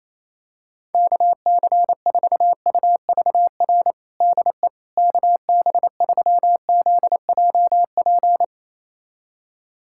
Morse code, heard as a keyed tone, about 28 wpm, 700 Hz. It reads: KC4UVR DE KB3ZJP
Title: USA Callsign DE Callsign 28wpm